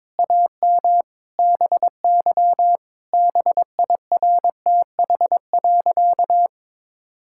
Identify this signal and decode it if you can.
Morse code, heard as a keyed tone, about 22 wpm, 700 Hz. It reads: AM BY BIRTH.
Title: Sentences from Top 1000 Words - Review Part 5 - 22wpm